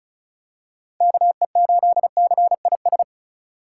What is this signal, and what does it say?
Morse code, heard as a keyed tone, about 34 wpm, 700 Hz. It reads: KE8CIS
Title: USA Call Signs - 6 Characters - 2x Character Spacing - 35wpm